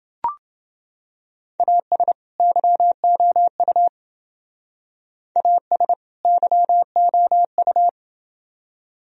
Morse code, heard as a keyed tone, about 28 wpm, 700 Hz. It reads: AS YOU AS YOU